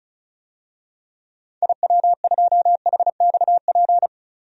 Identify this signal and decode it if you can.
Morse code, heard as a keyed tone, about 34 wpm, 700 Hz. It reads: IW2HXP
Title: DXCC-100 Call Signs - 6 Characters - 35wpm